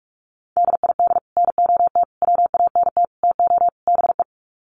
Morse code, heard as a keyed tone, about 44 wpm, 700 Hz. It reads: DID NOT WANT TO BE
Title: Sets of 5 Words 45wpm